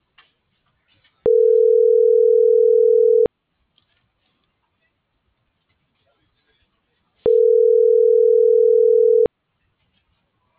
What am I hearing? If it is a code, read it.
Ringback tone (North America), 440 + 480 Hz.
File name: CENTRAL AGUILERA / out brenda.ramos